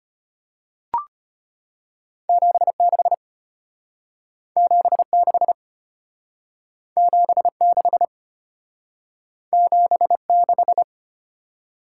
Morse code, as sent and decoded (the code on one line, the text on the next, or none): --... -.... / --... -.... / --... -.... / --... -....
76 76 76 76